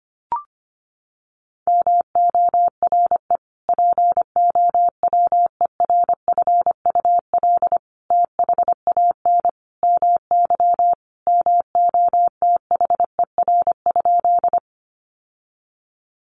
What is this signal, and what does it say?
Morse code, heard as a keyed tone, about 25 wpm, 700 Hz. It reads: MORE POWERFUL THAN MY MOTHER?
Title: Sentences from Top 1300 Words 25wpm